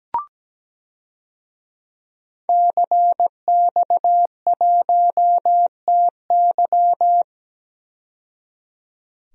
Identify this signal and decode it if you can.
Morse code, heard as a keyed tone, about 17 wpm, 700 Hz. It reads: CX1TY
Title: DXCC-100 Call Signs - 5 Characters - 17wpm